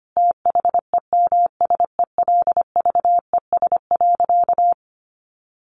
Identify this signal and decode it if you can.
Morse code, heard as a keyed tone, about 25 wpm, 700 Hz. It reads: THEMSELVES.
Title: Sentences from Top 1300 Words 25wpm